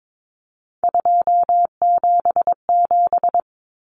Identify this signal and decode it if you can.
Morse code, heard as a keyed tone, about 22 wpm, 700 Hz. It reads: 277
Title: Common Cut Numbers - 3-Digit 22wpm